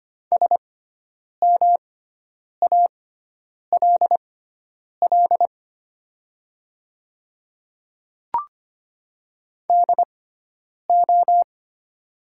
Morse code, heard as a keyed tone, about 25 wpm, spaced as for 9 wpm, 700 Hz. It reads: SMALL DO
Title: 6x Character Spacing - Top 200 Words - 25wpm